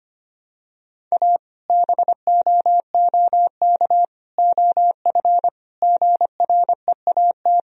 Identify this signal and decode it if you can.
Morse code, heard as a keyed tone, about 25 wpm, 700 Hz. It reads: A BOOK OF GREAT
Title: Sentences from Top 1000 Words - Review Part 2 - 25wpm